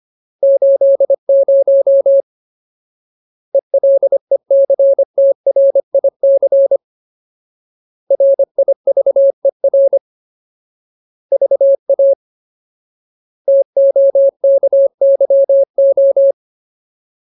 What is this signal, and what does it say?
Morse code, heard as a keyed tone, about 25 wpm, 550 Hz. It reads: 80 ELECTRIC RIVER VA TOKYO
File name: Ultimate Word List / 25WPM Ultimate